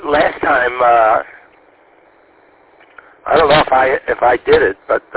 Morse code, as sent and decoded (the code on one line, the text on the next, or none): none